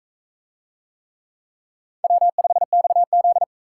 Morse code, heard as a keyed tone, about 39 wpm, 700 Hz. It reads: W5XZ